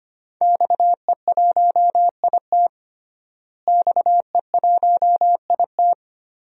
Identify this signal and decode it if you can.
Morse code, heard as a keyed tone, about 25 wpm, 700 Hz. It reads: XE1IT XE1IT
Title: DXCC-100 Call Signs - 5 Characters - Repeated 2x - 25wpm